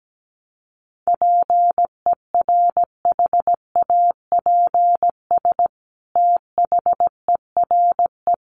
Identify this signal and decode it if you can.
Morse code, heard as a keyed tone, about 17 wpm, 700 Hz. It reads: PERHAPS THERE
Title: Sentences from Top 600 Words 17wpm